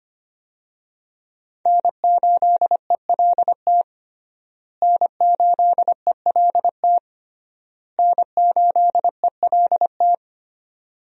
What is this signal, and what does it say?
Morse code, heard as a keyed tone, about 25 wpm, 700 Hz. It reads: N8ELT N8ELT N8ELT
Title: USA Call Signs - 5 Characters - Repeated 3x - 25wpm